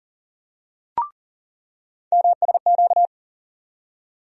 Morse code, heard as a keyed tone, about 40 wpm, 700 Hz. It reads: MSQ
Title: USA Call Sign Postfixes - 3 Letters 40wpm